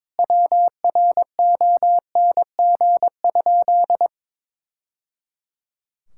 Morse code, heard as a keyed tone, about 22 wpm, 700 Hz. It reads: WRONG?
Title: Sentences from Top 1000 Words - Review Part 6 - 22wpm